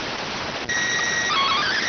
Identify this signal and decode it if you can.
SSTV (slow-scan television), PD 290.